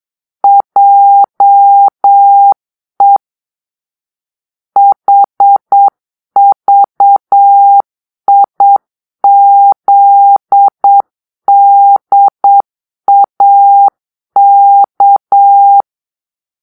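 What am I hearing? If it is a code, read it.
Morse code, heard as a keyed tone, about 8 wpm, 800 Hz. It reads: JE HVIZDAK
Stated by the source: Morse-code cipher made by a player team for the Kachna puzzle game.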